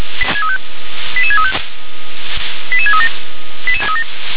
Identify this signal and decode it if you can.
Five-tone selective call (ZVEI), heard as quick runs of five tones, tones of about 70 ms. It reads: 88635 88637